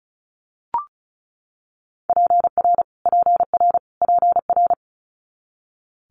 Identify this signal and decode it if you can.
Morse code, heard as a keyed tone, about 36 wpm, 700 Hz. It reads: PR PR PR
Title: Multi-letter Phonemes Repeated 3x 35wpm